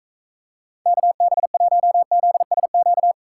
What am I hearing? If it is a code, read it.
Morse code, heard as a keyed tone, about 44 wpm, 700 Hz. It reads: KB1ZSQ